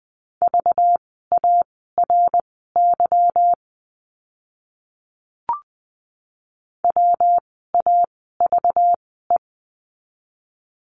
Morse code, heard as a keyed tone, about 20 wpm, spaced as for 15 wpm, 700 Hz. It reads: VARY WAVE
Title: 192 Lesson 31f - Letter v - Words 20wpm